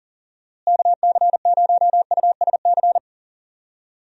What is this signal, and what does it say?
Morse code, heard as a keyed tone, about 39 wpm, 700 Hz. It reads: KC0USC
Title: USA Call Signs - 6 Characters - Repeated 3x - 40wpm